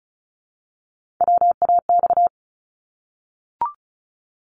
Morse code, heard as a keyed tone, about 36 wpm, 700 Hz. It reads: WAX
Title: Multi-letter Phonemes 35wpm